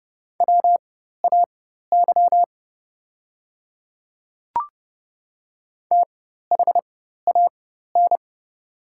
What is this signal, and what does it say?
Morse code, heard as a keyed tone, about 29 wpm, spaced as for 14 wpm, 700 Hz. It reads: WAY THAN
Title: 4x Character Spacing - Top 100 Words - 30wpm